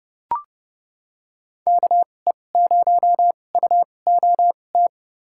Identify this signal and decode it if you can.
Morse code, heard as a keyed tone, about 30 wpm, spaced as for 22 wpm, 700 Hz. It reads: KE0UOT
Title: USA Call Signs - 6 Characters - 2x Character Spacing - 30wpm